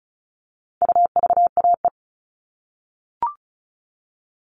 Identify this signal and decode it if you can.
Morse code, heard as a keyed tone, about 34 wpm, 700 Hz. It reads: UVAE
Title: All Cut Numbers - 4-Digit 35wpm